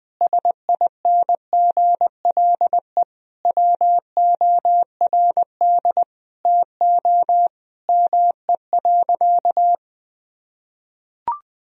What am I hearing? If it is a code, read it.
Morse code, heard as a keyed tone, about 20 wpm, 700 Hz. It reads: SINGLE WORD TO ME.
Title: Sentences from Top 1000 Words - Review Part 4 - 20wpm